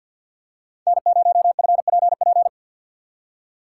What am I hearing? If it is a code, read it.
Morse code, heard as a keyed tone, about 53 wpm, 700 Hz. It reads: N0FPP
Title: USA Call Signs - 5 Characters - 6x Character Spacing - 50wpm